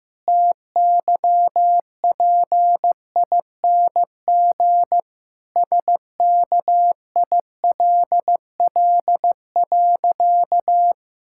Morse code, heard as a keyed tone, about 15 wpm, 700 Hz. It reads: TYPING SKILL.
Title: Pangram Challenge v2 - 15wpm